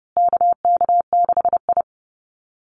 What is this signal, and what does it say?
Morse code, heard as a keyed tone, about 30 wpm, 700 Hz. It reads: KK6I
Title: Call Signs Repeated 2x - 30wpm